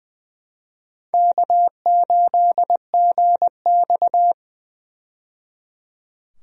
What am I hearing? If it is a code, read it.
Morse code, heard as a keyed tone, about 20 wpm, 700 Hz. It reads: K8GX
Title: USA Call Signs - 4 Characters - 20wpm